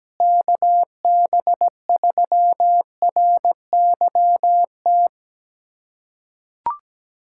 Morse code, heard as a keyed tone, about 17 wpm, 700 Hz. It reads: KB3RYT
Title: USA Callsign DE Callsign - 6 Characters - 17wpm